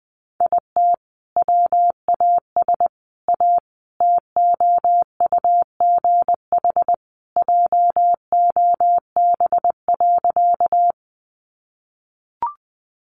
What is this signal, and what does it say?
Morse code, heard as a keyed tone, about 20 wpm, 700 Hz. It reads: IT WAS A TOUGH JOB.
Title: Sentences from Top 1200 Words 20wpm